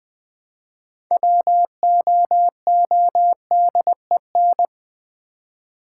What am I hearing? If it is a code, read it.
Morse code, heard as a keyed tone, about 20 wpm, 700 Hz. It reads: WOODEN
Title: Build A Word - 3 Letters At A Time - 6-Letter Words - 20wpm